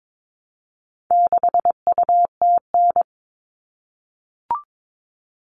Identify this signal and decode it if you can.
Morse code, heard as a keyed tone, about 22 wpm, 700 Hz. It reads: 6UTN
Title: All Cut Numbers - 4-Digit 22wpm